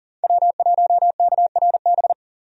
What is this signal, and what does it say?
Morse code, heard as a keyed tone, about 37 wpm, 700 Hz. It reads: W1KRB
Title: USA Call Signs - 5 Characters - Repeated 2x - 40wpm